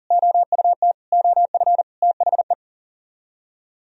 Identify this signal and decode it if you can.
Morse code, heard as a keyed tone, about 51 wpm, 700 Hz. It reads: OUT OF THE